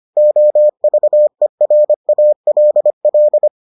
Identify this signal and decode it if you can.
Morse code, heard as a keyed tone, about 25 wpm, 600 Hz. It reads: OVERALL